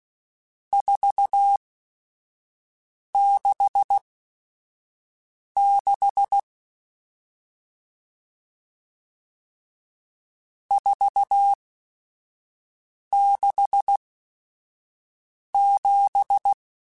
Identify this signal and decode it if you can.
Morse code, heard as a keyed tone, about 16 wpm, spaced as for 5 wpm, 800 Hz. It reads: 466 467
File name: RSTs / RST-mp3s / Various RST @5